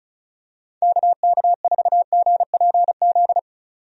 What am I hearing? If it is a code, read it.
Morse code, heard as a keyed tone, about 34 wpm, 700 Hz. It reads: KK4GPZ
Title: USA Call Signs - 6 Characters - 35wpm